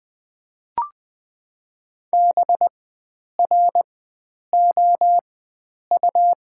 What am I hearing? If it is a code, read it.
Morse code, heard as a keyed tone, about 21 wpm, spaced as for 9 wpm, 700 Hz. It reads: BROU
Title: 4x Character Spacing - Top 500 Words - 20wpm